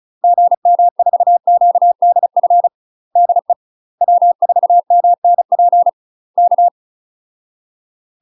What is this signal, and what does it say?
Morse code, heard as a keyed tone, about 34 wpm, 700 Hz. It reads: GM4QDF DE W4MNP K
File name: UK MASTER QSOs TRAINING / 35WPM / UK MASTER QSOs 0012